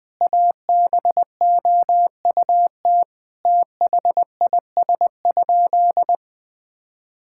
Morse code, heard as a keyed tone, about 20 wpm, 690 Hz. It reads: ABOUT THIS?
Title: Sentences with 4-Word Phrases - Spoken Beginning - 20wpm